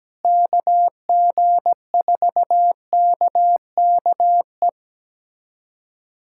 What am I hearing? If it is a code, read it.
Morse code, heard as a keyed tone, about 17 wpm, 700 Hz. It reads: KG4KKE
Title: USA Callsign DE Callsign - 6 Characters - 17wpm